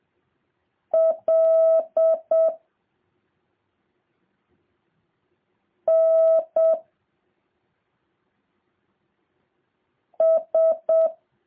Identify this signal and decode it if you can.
Morse code, heard as a keyed tone, about 7 wpm, 650 Hz. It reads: L N S